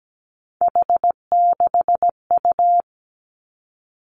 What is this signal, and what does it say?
Morse code, heard as a keyed tone, about 17 wpm, 700 Hz. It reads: H6U
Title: Three Challenging Letter-Number Combinations 17wpm